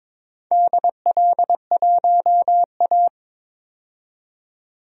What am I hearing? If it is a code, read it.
Morse code, heard as a keyed tone, about 22 wpm, 700 Hz. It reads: DL1A
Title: DXCC-100 Call Signs - 4 Characters - 22wpm